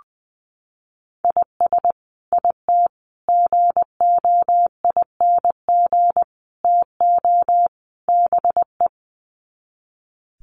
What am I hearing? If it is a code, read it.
Morse code, heard as a keyed tone, about 20 wpm, 700 Hz. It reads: IS IT GOING TO BE
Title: Sets of 5 Words Encore2 20wpm